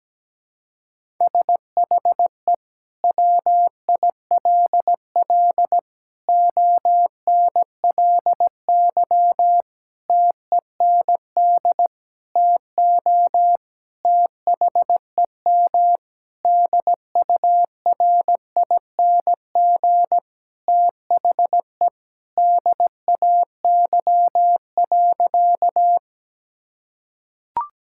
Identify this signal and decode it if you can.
Morse code, heard as a keyed tone, about 17 wpm, 700 Hz. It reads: SHE WILL ONLY TEND TO THEM DURING THE DAY.
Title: Sentences from Top 1000 Words - Review Part 5 - 17wpm